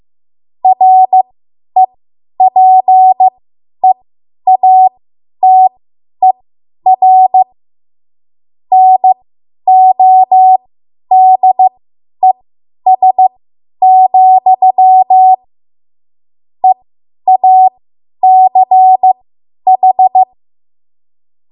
Morse code, heard as a keyed tone, about 15 wpm, spaced as for 10 wpm, 750 Hz. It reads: REPEATER NODES, EACH
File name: Morse / 10WPM